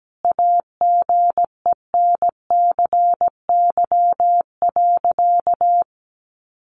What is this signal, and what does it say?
Morse code, heard as a keyed tone, about 17 wpm, 700 Hz. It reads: AGENCY.
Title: Sentences from Top 1300 Words 17wpm